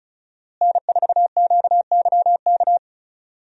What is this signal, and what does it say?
Morse code, heard as a keyed tone, about 35 wpm, 700 Hz. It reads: N4QYK
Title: USA Call Signs - 5 Characters - Repeated 2x - 35wpm